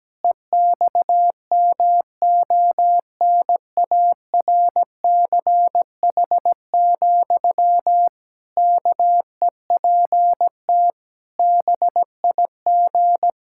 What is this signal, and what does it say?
Morse code, heard as a keyed tone, about 17 wpm, 700 Hz. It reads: EXMONARCH, KEPT BIG
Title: Pangram Challenge v2 - 17wpm